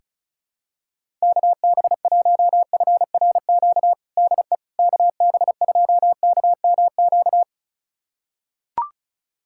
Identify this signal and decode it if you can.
Morse code, heard as a keyed tone, about 34 wpm, 700 Hz. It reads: KB1FRQ DE KB2KMQ
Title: USA Callsign DE Callsign - 6 Characters - 35wpm